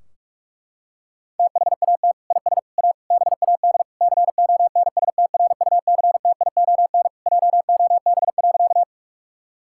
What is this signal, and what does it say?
Morse code, heard as a keyed tone, about 41 wpm, 700 Hz. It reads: THAT IS A BAD CONSTRUCTION JOB.
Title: Sentences from Top 1600 Words 45wpm